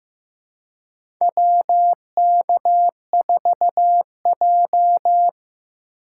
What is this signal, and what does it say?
Morse code, heard as a keyed tone, about 15 wpm, 700 Hz. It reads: WK4J
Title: USA Call Signs - 4x Character Spacing - 15wpm